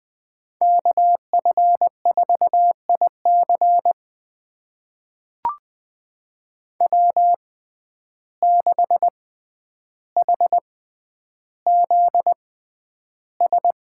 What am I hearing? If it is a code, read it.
Morse code, heard as a keyed tone, about 20 wpm, spaced as for 10 wpm, 700 Hz. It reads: KF4IC W6HZS
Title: USA Call Signs - 5 Characters - 6x Character Spacing - 20wpm